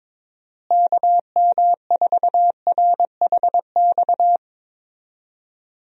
Morse code, heard as a keyed tone, about 22 wpm, 700 Hz. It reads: KM4RHX
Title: USA Call Signs - 6 Characters - Repeated 3x - 22wpm